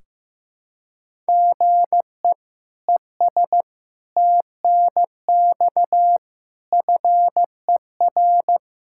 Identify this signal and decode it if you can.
Morse code, heard as a keyed tone, about 15 wpm, 700 Hz. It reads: GE ES TNX FER
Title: Practice QSO - Exchange 1 - Beginning - 15wpm